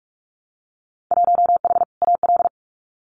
Morse code, heard as a keyed tone, about 46 wpm, 700 Hz. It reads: 1H AL
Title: Winter Field Day Exchanges 45wpm